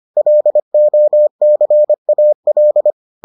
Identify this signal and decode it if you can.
Morse code, heard as a keyed tone, about 25 wpm, 600 Hz. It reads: LOCAL